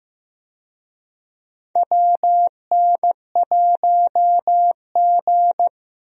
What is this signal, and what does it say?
Morse code, heard as a keyed tone, about 15 wpm, 700 Hz. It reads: WN1G
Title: USA Call Signs - 4 Characters - Repeated 3x - 15wpm